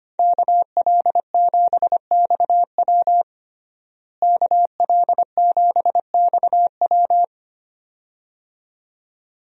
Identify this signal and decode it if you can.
Morse code, heard as a keyed tone, about 25 wpm, 700 Hz. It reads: KL7XW KL7XW
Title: DXCC-100 Call Signs - 5 Characters - Repeated 2x - 25wpm